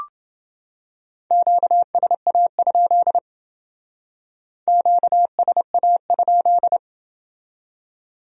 Morse code, heard as a keyed tone, about 28 wpm, 700 Hz. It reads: QSA? QSA?